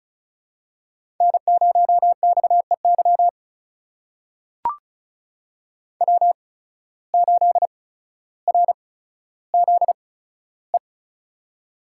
Morse code, heard as a keyed tone, about 36 wpm, spaced as for 15 wpm, 700 Hz. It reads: N0XEY W8RZE
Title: USA Call Signs - 5 Characters - 8x Character Spacing - 35wpm